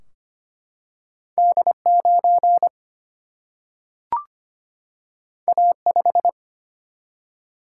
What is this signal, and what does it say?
Morse code, heard as a keyed tone, about 25 wpm, 700 Hz. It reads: D9 A5
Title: Single-Letter Single-Number 25wpm